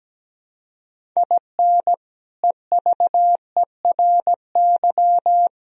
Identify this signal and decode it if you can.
Morse code, heard as a keyed tone, about 17 wpm, 700 Hz. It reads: IN EVERY
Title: Sentences from Top 1000 Words - Review Part 1 - 17wpm